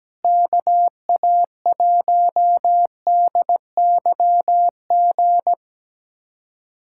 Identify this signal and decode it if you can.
Morse code, heard as a keyed tone, about 17 wpm, 700 Hz. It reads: KA1DYG
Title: USA Callsign DE Callsign - 6 Characters - 17wpm